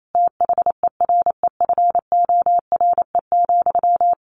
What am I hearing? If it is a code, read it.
Morse code, heard as a keyed tone, about 28 wpm, 700 Hz. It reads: THEREFORE,
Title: Sentences from Top 1400 Words 28wpm